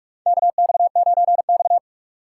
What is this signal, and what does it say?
Morse code, heard as a keyed tone, about 47 wpm, 700 Hz. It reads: KX9X
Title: USA Call Signs - 6x Character Spacing - 45wpm